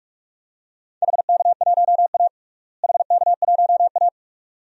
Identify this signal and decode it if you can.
Morse code, heard as a keyed tone, about 42 wpm, 700 Hz. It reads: HK1A HK1A
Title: DXCC-100 Call Signs - 4 Characters - Repeated 2x - 45wpm